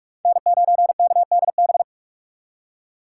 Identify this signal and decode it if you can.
Morse code, heard as a keyed tone, about 46 wpm, 700 Hz. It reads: N9KDB
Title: K1USN SST QSO Practice - 45wpm